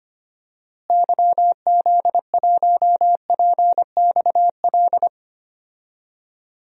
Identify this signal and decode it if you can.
Morse code, heard as a keyed tone, about 25 wpm, 700 Hz. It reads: YZ1PXL
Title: DXCC-100 Call Signs - 6 Characters - Repeated 3x - 25wpm